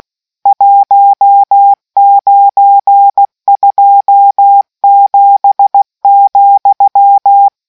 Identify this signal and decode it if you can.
Morse code, heard as a keyed tone, about 16 wpm, 800 Hz. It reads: 1927,